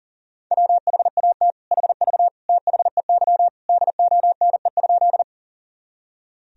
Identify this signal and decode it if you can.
Morse code, heard as a keyed tone, about 40 wpm, 700 Hz. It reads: WHAT HV THEY DONE?
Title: Sentences from Top 500 Words and Abbreviations 40wpm